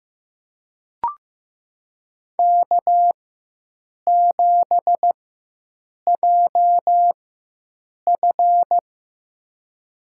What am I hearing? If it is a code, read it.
Morse code, heard as a keyed tone, about 15 wpm, spaced as for 7 wpm, 700 Hz. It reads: K7JF
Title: USA Call Signs - 4 Characters - 4x Character Spacing - 15wpm